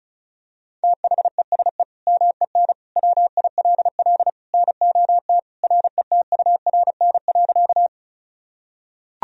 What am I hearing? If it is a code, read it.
Morse code, heard as a keyed tone, about 34 wpm, 700 Hz. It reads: THESE MEN WILL NOT RETURN.
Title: Sentences from Top 1000 Words - Review Part 1 - 35wpm